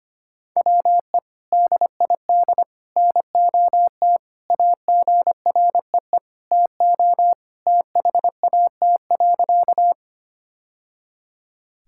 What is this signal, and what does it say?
Morse code, heard as a keyed tone, about 25 wpm, 700 Hz. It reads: WE DID NOT AGREE TO THAT.
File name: practice / 25wpm / Sentences with 3-Word Phrases - 25wpm